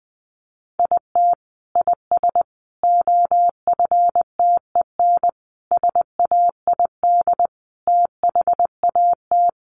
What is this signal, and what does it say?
Morse code, heard as a keyed tone, about 20 wpm, 700 Hz. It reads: IT IS OFTEN SAID THAT
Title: Sentences from Top 1400 Words 20wpm